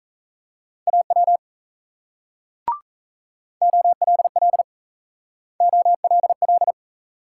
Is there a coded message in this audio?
Yes: Morse code from 0.9 to 6.7 s, heard as a keyed tone, about 36 wpm, 700 Hz. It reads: AW OLL OLL